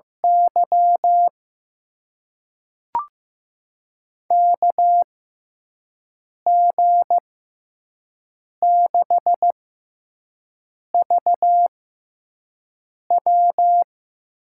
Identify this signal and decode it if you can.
Morse code, heard as a keyed tone, about 15 wpm, spaced as for 5 wpm, 700 Hz. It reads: Y KG6VW
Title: USA Call Signs - 6 Characters - 6x Character Spacing - 15wpm